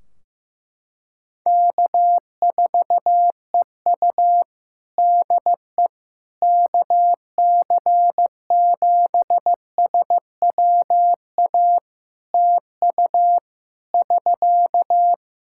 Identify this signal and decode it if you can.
Morse code, heard as a keyed tone, about 15 wpm, 700 Hz. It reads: K4EU DE KC7SWA TU <SK>
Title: Practice QSO - Exchange 4 - Ending - 15wpm